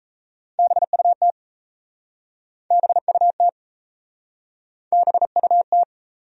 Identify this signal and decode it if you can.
Morse code, heard as a keyed tone, about 37 wpm, 700 Hz. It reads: BUT BUT BUT